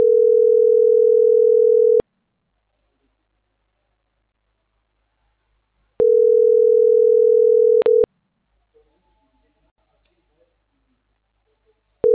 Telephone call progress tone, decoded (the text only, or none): ringback tone